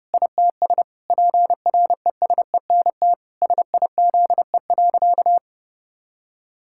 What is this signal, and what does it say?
Morse code, heard as a keyed tone, about 30 wpm, 700 Hz. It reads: ITS PRESENT SIZE.
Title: Sentences from Top 1000 Words - Review Part 6 - 30wpm